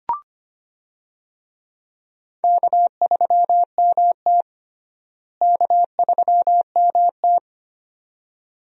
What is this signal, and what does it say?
Morse code, heard as a keyed tone, about 25 wpm, 700 Hz. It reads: K3MT K3MT
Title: USA Call Signs - 4 Characters - Repeated 2x - 25wpm